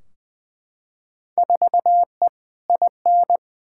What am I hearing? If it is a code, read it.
Morse code, heard as a keyed tone, about 20 wpm, 700 Hz. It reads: 4E IN
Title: Field Day Exchanges - 20wpm